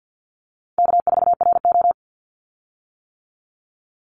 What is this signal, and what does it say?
Morse code, heard as a keyed tone, about 50 wpm, 700 Hz. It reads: K4RO
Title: Call Signs - 50wpm